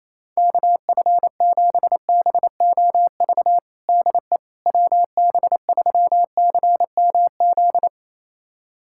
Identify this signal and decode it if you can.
Morse code, heard as a keyed tone, about 28 wpm, 700 Hz. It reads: KF7BOV DE WB3CMZ
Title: USA Callsign DE Callsign 28wpm